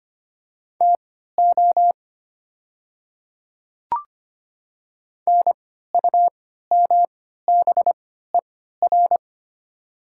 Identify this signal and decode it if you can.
Morse code, heard as a keyed tone, about 25 wpm, spaced as for 14 wpm, 700 Hz. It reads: TO NUMBER
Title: 3x Character Spacing - Top 100 Words - 25wpm